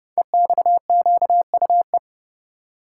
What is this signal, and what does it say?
Morse code, heard as a keyed tone, about 30 wpm, 700 Hz. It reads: EXQUE
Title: Code Groups - 5 Letter:Numbers - 30wpm